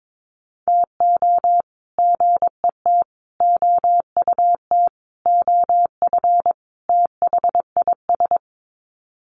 Morse code, heard as a keyed tone, about 22 wpm, 700 Hz. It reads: TO GET OUT OF THIS
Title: Sets of 5 Words Encore2 22wpm